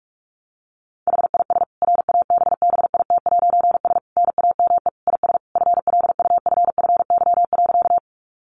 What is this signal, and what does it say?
Morse code, heard as a keyed tone, about 44 wpm, 700 Hz. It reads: HIS RABBIT'S NAME IS FLUFFY.
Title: Easy Phrases 45wpm